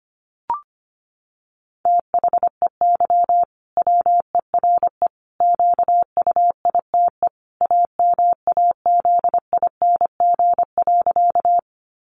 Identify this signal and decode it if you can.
Morse code, heard as a keyed tone, about 25 wpm, 700 Hz. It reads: THEY WERE QUITE AMAZING.
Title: Sentences from Top 1500 Words 25wpm